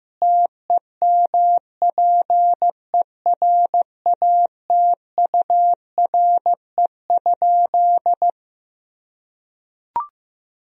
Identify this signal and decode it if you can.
Morse code, heard as a keyed tone, about 15 wpm, 700 Hz. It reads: TEMPERATURE?
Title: Sentences from Top 1000 Words - Review Part 4 - 15wpm